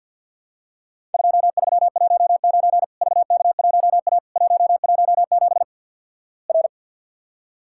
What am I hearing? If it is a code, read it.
Morse code, heard as a keyed tone, about 50 wpm, 700 Hz. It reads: 2319 VK1A 117 R
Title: Worked All Europe DX Contest - QTC Exchange - 50wpm